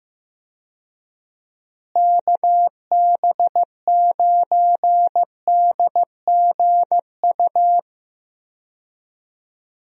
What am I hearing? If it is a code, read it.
Morse code, heard as a keyed tone, about 15 wpm, 700 Hz. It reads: KB9DGU
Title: USA Call Signs - 6 Characters - Repeated 3x - 15wpm